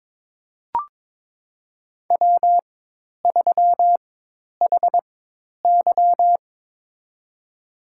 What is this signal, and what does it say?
Morse code, heard as a keyed tone, about 22 wpm, spaced as for 10 wpm, 700 Hz. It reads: W3HY
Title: USA Call Signs - 4 Characters - 4x Character Spacing - 22wpm